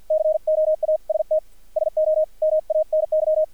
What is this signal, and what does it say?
Morse code, heard as a keyed tone, about 50 wpm, spaced as for 44 wpm, 620 Hz, with radio noise. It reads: COAST SO MANY
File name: archived / CRSnet